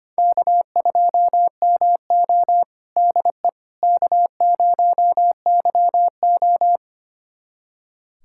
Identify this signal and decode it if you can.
Morse code, heard as a keyed tone, about 25 wpm, 700 Hz. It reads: K2MO DE K0YO
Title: USA Callsign DE Callsign - 4 Characters - 25wpm